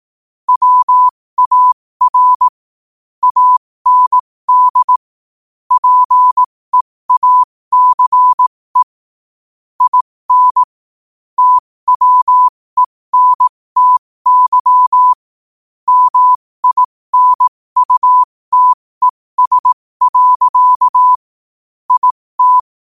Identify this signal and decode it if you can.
Morse code, heard as a keyed tone, about 18 wpm, spaced as for 15 wpm, 1000 Hz. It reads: WAR AND PEACE IN TWENTY MINUTES. IT